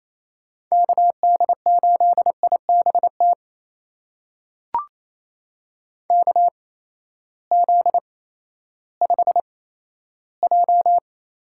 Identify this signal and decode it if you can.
Morse code, heard as a keyed tone, about 28 wpm, spaced as for 14 wpm, 700 Hz. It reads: KD8IBT KZ5J
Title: USA Call Signs - 6 Characters - 8x Character Spacing - 28wpm